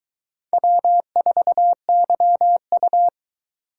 Morse code, heard as a keyed tone, about 21 wpm, 700 Hz. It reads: W4YU